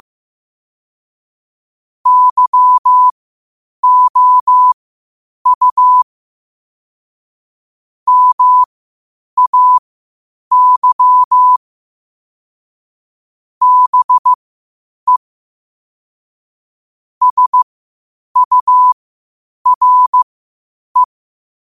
Morse code, heard as a keyed tone, about 15 wpm, spaced as for 8 wpm, 1000 Hz. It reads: YOU MAY BE SURE
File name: qotdfiles / 8wpm